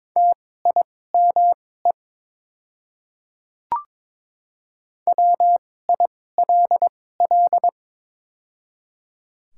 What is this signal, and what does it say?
Morse code, heard as a keyed tone, about 22 wpm, spaced as for 16 wpm, 700 Hz. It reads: TIME WILL
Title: 2x Character Spacing - Top 100 Words - 22wpm